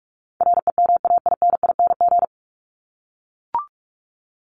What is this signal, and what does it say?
Morse code, heard as a keyed tone, about 44 wpm, 700 Hz. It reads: REMAINING
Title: ING Suffix - Top 100 Words with Base Words 45wpm